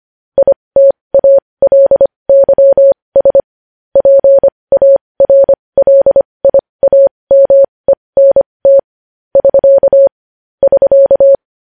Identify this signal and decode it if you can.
Morse code, heard as a keyed tone, about 25 wpm, spaced as for 20 wpm, 550 Hz. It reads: ITALYS PARLIAMENT <SK> <SK>
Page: news headlines in morse code at 20 wpm